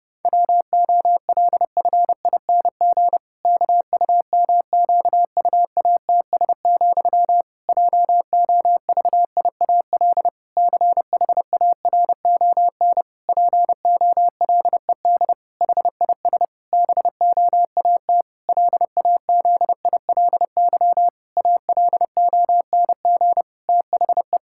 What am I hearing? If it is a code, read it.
Morse code, heard as a keyed tone, about 30 wpm, 700 Hz. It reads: WOLFING KUMQUATS, JOVIAL CHARON POLED HIS BOAT LAZILY ALONG THE
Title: Pangram Challenge v2 - 30wpm